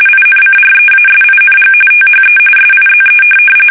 RTTY, about 45.45 baud. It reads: INCREASING THE SIZE
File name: fsk850 / USS-Massachusetts